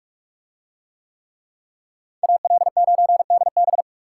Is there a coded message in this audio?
Yes: Morse code from 2.2 to 3.8 s, heard as a keyed tone, about 46 wpm, 700 Hz. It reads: AL9DB